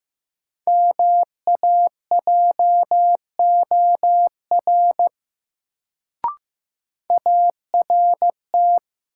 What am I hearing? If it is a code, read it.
Morse code, heard as a keyed tone, about 15 wpm, 700 Hz. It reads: MAJOR ART
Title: Sentences from Top 600 Words - New Word Drill - 15wpm